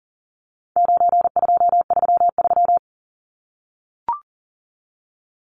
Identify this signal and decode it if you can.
Morse code, heard as a keyed tone, about 40 wpm, 700 Hz. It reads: 9233
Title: Numbers - 4-Digit 40wpm